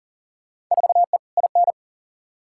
Morse code, heard as a keyed tone, about 40 wpm, 700 Hz. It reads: 4E IN
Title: Field Day Exchanges - 40wpm